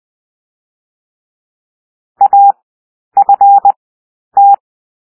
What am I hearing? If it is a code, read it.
Morse code, heard as a keyed tone, about 20 wpm, 800 Hz. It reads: A F T
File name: cwmp3 / W1 SOUND WORDS 20 wpm